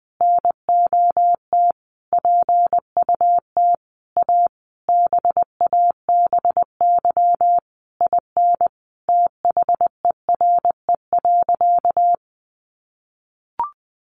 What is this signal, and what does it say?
Morse code, heard as a keyed tone, about 20 wpm, 700 Hz. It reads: NOT PUT A BABY IN THERE.
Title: Easy Phrases 20wpm